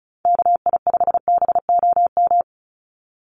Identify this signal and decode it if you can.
Morse code, heard as a keyed tone, about 34 wpm, 700 Hz. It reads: KI5BOM
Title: USA Call Signs 35wpm